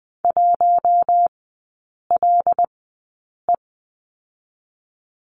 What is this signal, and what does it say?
Morse code, heard as a keyed tone, about 20 wpm, spaced as for 8 wpm, 700 Hz. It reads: 1LE
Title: 074 Lesson 13e - Letter L - Three Characters 20wpm